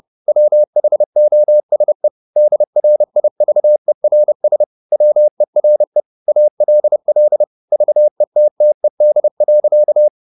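Morse code, heard as a keyed tone, about 30 wpm, 600 Hz. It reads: WHOSE DRIVERS WERE ALL VETTED.
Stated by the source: Morse code at 30 wpm, the novel Never by Ken Follett in English, from CW med Gnister.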